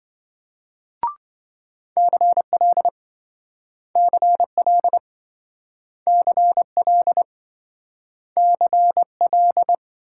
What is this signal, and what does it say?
Morse code, heard as a keyed tone, about 26 wpm, 700 Hz. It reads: CL CL CL CL